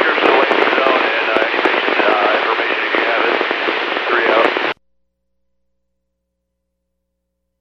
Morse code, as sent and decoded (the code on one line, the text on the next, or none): none